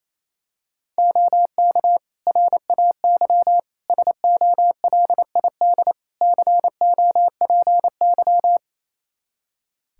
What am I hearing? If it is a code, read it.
Morse code, heard as a keyed tone, about 28 wpm, 700 Hz. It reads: OK RAY SOLID COPY
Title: Practice QSO - Exchange 3 - More Info - 28wpm